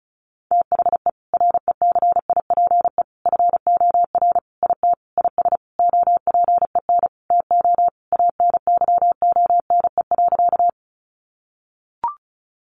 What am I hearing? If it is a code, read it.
Morse code, heard as a keyed tone, about 34 wpm, 700 Hz. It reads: THE RECIPE FOR IT IS OPEN TO ANYONE.
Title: Sentences from Top 1200 Words 35wpm